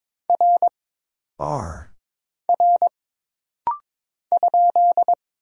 Morse code, heard as a keyed tone, about 22 wpm, 700 Hz. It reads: R R ?
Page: This rapid-fire timing is geared to push your ICR (Instant Character Recognition) skills.